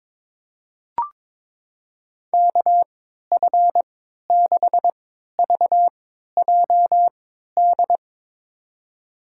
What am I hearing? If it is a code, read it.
Morse code, heard as a keyed tone, about 22 wpm, spaced as for 12 wpm, 700 Hz. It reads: KF6VJD
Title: USA Call Signs - 6 Characters - 3x Character Spacing - 22wpm